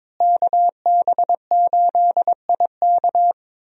Morse code, heard as a keyed tone, about 22 wpm, 700 Hz. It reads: KB8IK
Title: USA Call Signs - 5 Characters - Repeated 3x - 22wpm